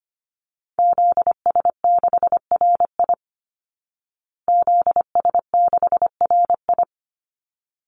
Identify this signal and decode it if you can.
Morse code, heard as a keyed tone, about 25 wpm, 700 Hz. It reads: ZS6RI ZS6RI
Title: Call Signs Repeated 3x - 25wpm